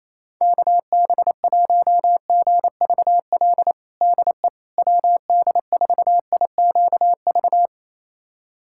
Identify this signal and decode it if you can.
Morse code, heard as a keyed tone, about 28 wpm, 700 Hz. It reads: KB1GVL DE WD4IQV
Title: USA Callsign DE Callsign - 6 Characters - 28wpm